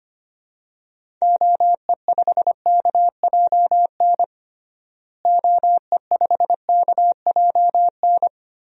Morse code, heard as a keyed tone, about 25 wpm, 700 Hz. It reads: OE5KJN OE5KJN
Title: DXCC-100 Call Signs - 6 Characters - Repeated 3x - 25wpm